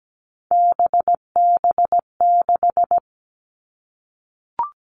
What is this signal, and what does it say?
Morse code, heard as a keyed tone, about 17 wpm, 700 Hz. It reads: BB6
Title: All Cut Numbers - 3-Digit 17wpm